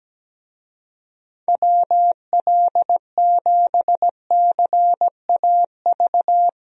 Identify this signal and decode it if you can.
Morse code, heard as a keyed tone, about 17 wpm, 700 Hz. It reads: WL7CAV
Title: USA Call Signs - 6 Characters - Repeated 3x - 17wpm